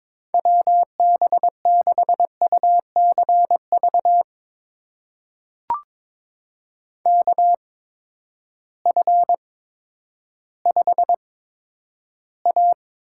Morse code, heard as a keyed tone, about 22 wpm, spaced as for 11 wpm, 700 Hz. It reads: WB6UCV KF5A
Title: USA Call Signs - 6 Characters - 8x Character Spacing - 22wpm